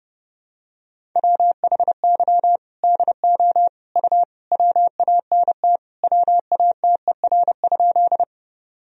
Morse code, heard as a keyed tone, about 30 wpm, 700 Hz. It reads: WHY DO U WANT WATER?
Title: Sentences from Top 200 Words and Abbreviations 30wpm